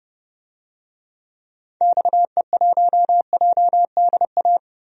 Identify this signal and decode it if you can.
Morse code, heard as a keyed tone, about 30 wpm, 700 Hz. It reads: XE1JDA
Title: DXCC-100 Call Signs - 6 Characters - 30wpm